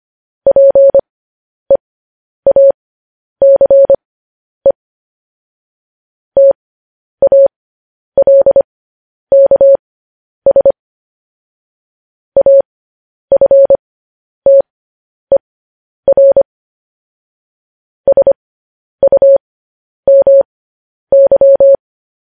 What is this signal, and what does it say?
Morse code, heard as a keyed tone, about 25 wpm, spaced as for 10 wpm, 550 Hz. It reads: PEACE TALKS AFTER SUMY
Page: news-headlines-in-morse-code-at-10-wpm